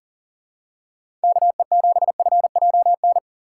Morse code, heard as a keyed tone, about 40 wpm, 700 Hz. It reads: KE7FJN
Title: USA Call Signs - 6 Characters - 4x Character Spacing - 40wpm